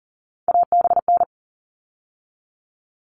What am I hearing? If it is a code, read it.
Morse code, heard as a keyed tone, about 40 wpm, 700 Hz. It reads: ABN
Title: All Cut Numbers - 3-Digit 40wpm